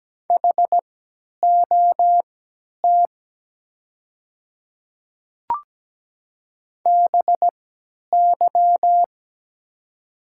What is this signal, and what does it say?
Morse code, heard as a keyed tone, about 17 wpm, spaced as for 10 wpm, 700 Hz. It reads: HOT BY
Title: 3x Character Spacing - Top 200 Words - 17wpm